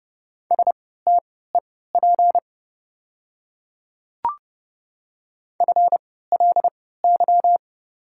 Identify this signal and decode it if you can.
Morse code, heard as a keyed tone, about 30 wpm, spaced as for 17 wpm, 690 Hz. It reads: STEP FLY
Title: 3x Character Spacing - Top 500 Words - 30wpm